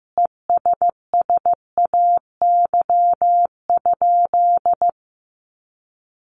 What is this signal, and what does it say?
Morse code, heard as a keyed tone, about 15 wpm, 700 Hz. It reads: ESSAY?
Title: Sentences from Top 1500 Words 15wpm